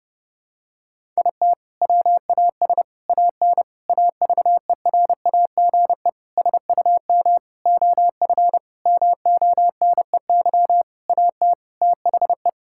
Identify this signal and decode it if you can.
Morse code, heard as a keyed tone, about 32 wpm, 700 Hz. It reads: IT WAS AN AVERAGE SUM OF MONEY AT THE
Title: Sentences from Top 1000 Words - Review Part 6 - 30wpm